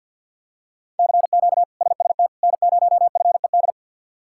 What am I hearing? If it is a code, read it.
Morse code, heard as a keyed tone, about 50 wpm, 700 Hz. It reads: CQ SST N0FED
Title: K1USN SST QSO Practice - 50wpm